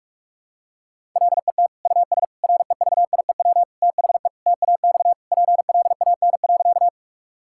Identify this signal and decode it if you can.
Morse code, heard as a keyed tone, about 41 wpm, 700 Hz. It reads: LET US REVIEW THE TAX PLAN.
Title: Sentences from Top 1000 Words - Review Part 6 - 45wpm